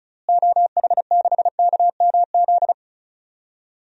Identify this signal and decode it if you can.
Morse code, heard as a keyed tone, about 34 wpm, 700 Hz. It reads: OH6KMZ
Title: DXCC-100 Call Signs - 6 Characters - Repeated 3x - 35wpm